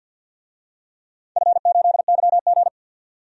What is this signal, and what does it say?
Morse code, heard as a keyed tone, about 51 wpm, 700 Hz. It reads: F8YG
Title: DXCC-100 Call Signs - 4 Characters - 50wpm